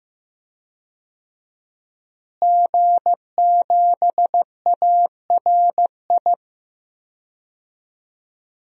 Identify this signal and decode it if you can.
Morse code, heard as a keyed tone, about 15 wpm, 700 Hz. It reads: G7ARI
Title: DXCC-100 Call Signs - 5 Characters - Repeated 3x - 15wpm